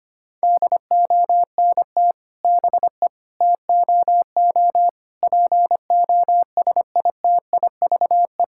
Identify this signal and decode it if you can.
Morse code, heard as a keyed tone, about 25 wpm, 700 Hz. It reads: DONT BE TOO POSITIVE
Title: Sentences from Top 1000 Words - Review Part 5 - 25wpm